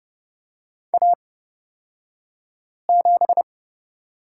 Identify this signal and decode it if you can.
Morse code, heard as a keyed tone, about 31 wpm, spaced as for 5 wpm, 700 Hz. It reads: A7
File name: Lesson 37d - Letter x - Two Characters 30wpm